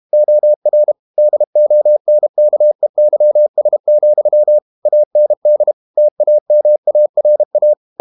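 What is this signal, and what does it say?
Morse code, heard as a keyed tone, about 32 wpm, 600 Hz. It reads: OR DONKEYS, AND TAMARA
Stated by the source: Morse code practice at 32 wpm from CW med Gnister, reading Ken Follett's novel Never in English.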